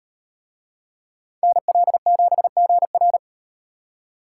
Morse code, heard as a keyed tone, about 36 wpm, 700 Hz. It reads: NL7GR